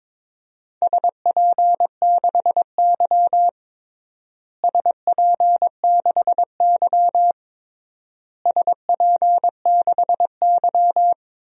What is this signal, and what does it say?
Morse code, heard as a keyed tone, about 21 wpm, 700 Hz. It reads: SP6Y SP6Y SP6Y
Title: DXCC-100 Call Signs - 4 Characters - Repeated 3x - 22wpm